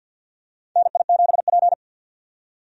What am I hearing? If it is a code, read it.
Morse code, heard as a keyed tone, about 51 wpm, 700 Hz. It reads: NI7P
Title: USA Call Signs - 4 Characters - 50wpm